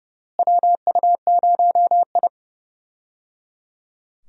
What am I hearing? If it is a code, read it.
Morse code, heard as a keyed tone, about 30 wpm, 700 Hz. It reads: WU0I
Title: USA Call Signs - 4 Characters - Repeated 3x - 30wpm